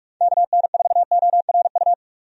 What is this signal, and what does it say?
Morse code, heard as a keyed tone, about 46 wpm, 700 Hz. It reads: KN4ORU
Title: USA Call Signs - 6 Characters - 45wpm